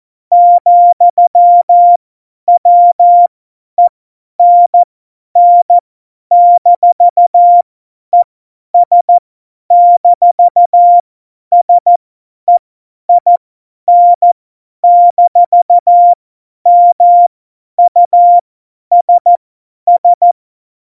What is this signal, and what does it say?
Morse code, heard as a keyed tone, about 14 wpm, spaced as for 10 wpm, 700 Hz. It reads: ,WENN-ES-SEIN-MUSS
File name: Morsekurs / Witze / Witz 13. August 2019, 50 F70